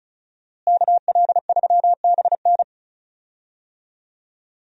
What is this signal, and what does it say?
Morse code, heard as a keyed tone, about 36 wpm, 700 Hz. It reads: KL3BN
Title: USA Call Signs - 5 Characters - Repeated 3x - 35wpm